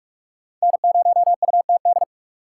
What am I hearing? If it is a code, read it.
Morse code, heard as a keyed tone, about 42 wpm, 700 Hz. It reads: N0UTD